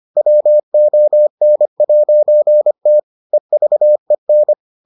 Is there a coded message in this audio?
Yes: Morse code from 0.2 to 4.5 s, heard as a keyed tone, about 25 wpm, 600 Hz. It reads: WON'T EVEN